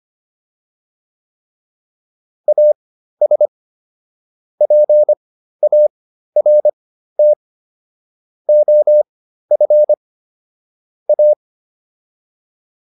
Morse code, heard as a keyed tone, about 25 wpm, spaced as for 13 wpm, 600 Hz. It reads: AS PART OF A